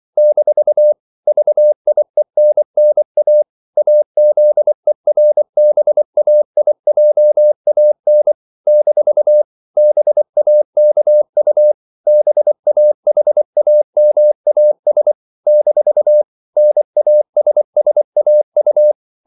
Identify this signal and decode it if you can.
Morse code, heard as a keyed tone, about 24 wpm, 600 Hz. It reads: - VIENNA AZERBAIJAN - BAKU BAHAMAS - NASSAU
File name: CW / Practice / NationalCapitals24